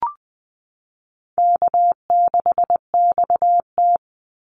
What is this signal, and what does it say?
Morse code, heard as a keyed tone, about 20 wpm, 700 Hz. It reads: K6XT
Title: Easy Call Signs - 20wpm